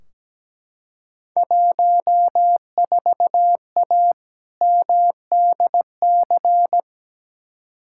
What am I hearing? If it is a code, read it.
Morse code, heard as a keyed tone, about 17 wpm, 700 Hz. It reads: 14A MDC
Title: Field Day Exchanges Encore - 17wpm